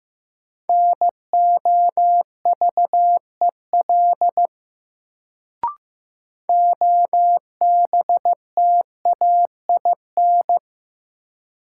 Morse code, heard as a keyed tone, about 15 wpm, 700 Hz. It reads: NOVEL OBTAIN
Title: Sentences from Top 1300 Words - New Word Drill - 15wpm